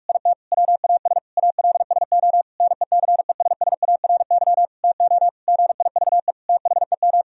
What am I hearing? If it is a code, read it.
Morse code, heard as a keyed tone, about 41 wpm, 700 Hz. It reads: IT WAS ALSO NECESSARY TO GIVE THEM
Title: Sentences from Top 1000 Words - Review Part 2 - 45wpm